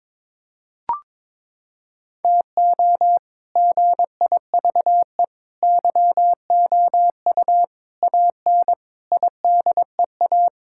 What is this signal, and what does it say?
Morse code, heard as a keyed tone, about 21 wpm, 700 Hz. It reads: TO GIVE YOU AN IDEA
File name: practice / 22wpm / Sets of 6 Words - 22wpm